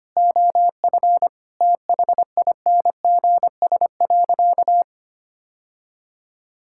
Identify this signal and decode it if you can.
Morse code, heard as a keyed tone, about 25 wpm, 700 Hz. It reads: OF THINGS.
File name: practice / 25wpm / Sentences with 4-Word Phrases - 25wpm